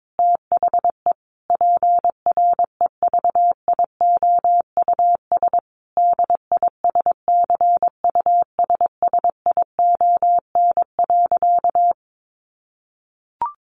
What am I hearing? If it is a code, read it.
Morse code, heard as a keyed tone, about 22 wpm, 700 Hz. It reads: THE PREVIOUS DISCUSSION.
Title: Sentences from Top 1500 Words 22wpm